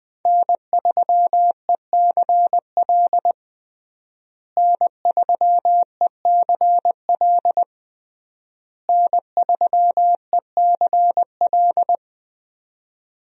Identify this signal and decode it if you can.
Morse code, heard as a keyed tone, about 21 wpm, 700 Hz. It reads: N3ECL N3ECL N3ECL
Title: USA Call Signs - 5 Characters - Repeated 3x - 20wpm